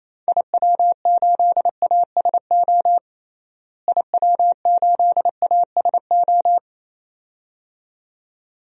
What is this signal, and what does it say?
Morse code, heard as a keyed tone, about 28 wpm, 700 Hz. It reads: IW8ASO IW8ASO
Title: DXCC-100 Call Signs - 6 Characters - Repeated 2x - 28wpm